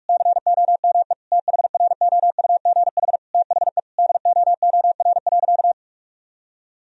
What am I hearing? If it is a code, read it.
Morse code, heard as a keyed tone, about 41 wpm, 700 Hz. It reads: COME THROUGH THE DOOR.
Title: Sentences from Top 1000 Words - Review Part 1 - 45wpm